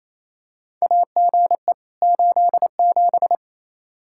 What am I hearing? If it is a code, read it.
Morse code, heard as a keyed tone, about 28 wpm, 700 Hz. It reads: AGE 87
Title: K1USN SST QSO Practice 28wpm